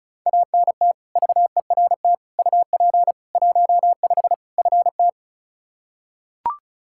Morse code, heard as a keyed tone, about 36 wpm, 700 Hz. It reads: ANT VERT UP 15 FT
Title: Practice QSO - Exchange 2 - Info - 35wpm